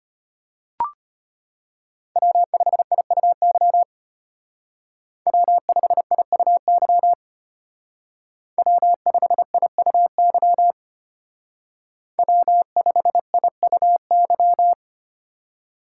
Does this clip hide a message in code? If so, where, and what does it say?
Yes: Morse code from 2.2 to 14.7 s, heard as a keyed tone, about 32 wpm, 700 Hz. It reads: W5IUY W5IUY W5IUY W5IUY